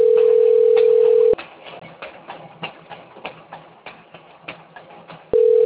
Ringback tone (North America), 440 + 480 Hz.